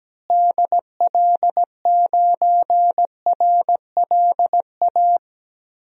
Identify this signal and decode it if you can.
Morse code, heard as a keyed tone, about 17 wpm, 700 Hz. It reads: DL9RLA
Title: DXCC-100 Call Signs - 6 Characters - Repeated 3x - 17wpm